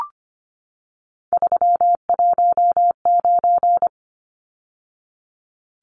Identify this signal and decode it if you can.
Morse code, heard as a keyed tone, about 25 wpm, 700 Hz. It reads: 319
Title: Numbers - 3-Digit 25wpm